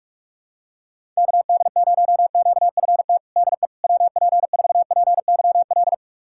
Morse code, heard as a keyed tone, about 46 wpm, 700 Hz. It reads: KD0QFT DE WP4PYL
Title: USA Callsign DE Callsign - 6 Characters - 45wpm